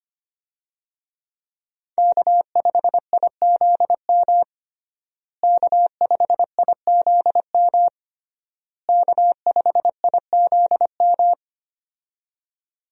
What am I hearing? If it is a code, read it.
Morse code, heard as a keyed tone, about 25 wpm, 700 Hz. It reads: K5IZM K5IZM K5IZM
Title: USA Call Signs - 5 Characters - Repeated 3x - 25wpm